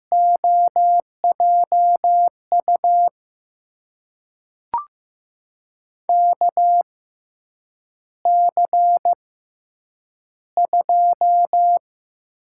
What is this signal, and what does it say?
Morse code, heard as a keyed tone, about 15 wpm, spaced as for 8 wpm, 700 Hz. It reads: OJU KC2
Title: USA Call Signs - 6 Characters - 6x Character Spacing - 15wpm